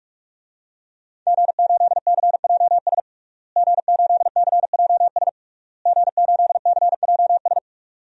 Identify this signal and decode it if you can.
Morse code, heard as a keyed tone, about 41 wpm, 700 Hz. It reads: G8CJS G8CJS G8CJS
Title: DXCC-100 Call Signs - 5 Characters - Repeated 3x - 45wpm